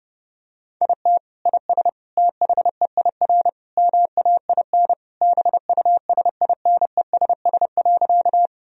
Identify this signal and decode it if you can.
Morse code, heard as a keyed tone, about 30 wpm, 700 Hz. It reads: IT IS THEIR MAIN BUSINESS.
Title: Sentences from Top 1000 Words - Review Part 2 - 30wpm